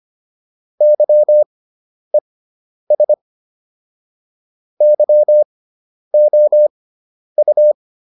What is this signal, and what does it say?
Morse code, heard as a keyed tone, about 25 wpm, spaced as for 10 wpm, 600 Hz. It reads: YES YOU